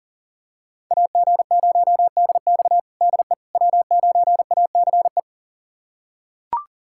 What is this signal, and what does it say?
Morse code, heard as a keyed tone, about 40 wpm, 700 Hz. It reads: AG0DX DE W9ACE
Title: USA Callsign DE Callsign - 5 Characters - 40wpm